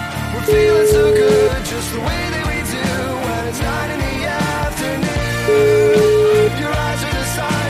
Ringback tone (Europe, 425 Hz), with noise.